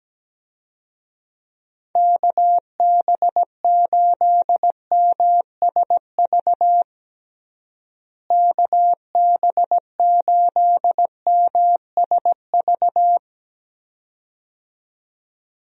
Morse code, heard as a keyed tone, about 17 wpm, 700 Hz. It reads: KB8MSV KB8MSV
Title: USA Call Signs - 6 Characters - Repeated 2x - 17wpm